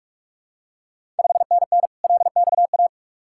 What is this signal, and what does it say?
Morse code, heard as a keyed tone, about 46 wpm, 690 Hz. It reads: 5NN LXA
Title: Ontario QSO Party Exchange 45wpm